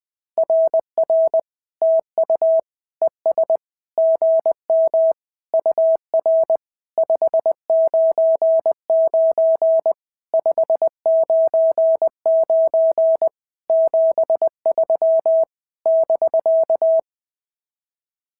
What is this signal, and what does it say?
Morse code, heard as a keyed tone, about 19 wpm, 650 Hz. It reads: RR TU ES GM UR 599 599 73 <BK>
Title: SOTA QSO Practice 20wpm